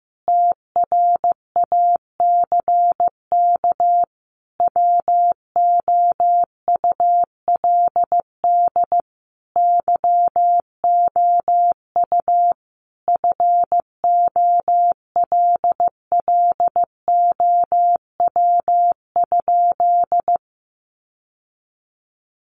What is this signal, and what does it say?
Morse code, heard as a keyed tone, about 15 wpm, 700 Hz. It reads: TRACK WOULD YOU FOLLOW?
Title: Sentences from Top 1100 Words 15wpm